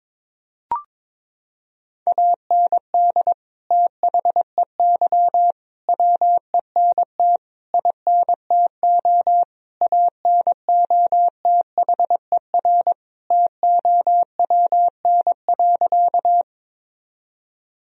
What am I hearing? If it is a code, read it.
Morse code, heard as a keyed tone, about 22 wpm, 700 Hz. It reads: AND THEY WENT INTO ANOTHER TOWN.
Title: Sentences from Top 1000 Words - Review Part 2 - 22wpm